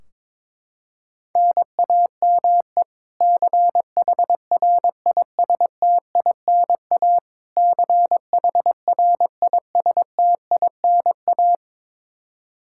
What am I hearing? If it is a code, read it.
Morse code, heard as a keyed tone, about 22 wpm, 700 Hz. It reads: NAME CHRISTINA CHRISTINA
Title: K1USN SST QSO Practice 22wpm